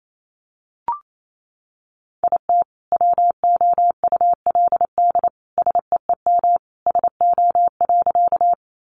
Morse code, heard as a keyed tone, about 28 wpm, 700 Hz. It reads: IT WOULD SEEM SO.
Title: Sentences from Top 300 Words 28wpm